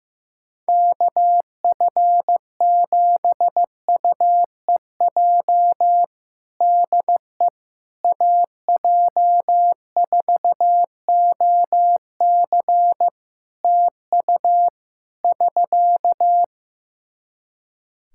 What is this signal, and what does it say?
Morse code, heard as a keyed tone, about 15 wpm, 700 Hz. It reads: KF7UEJ DE AJ4OC TU <SK>
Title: Practice QSO - Exchange 4 - Ending - 15wpm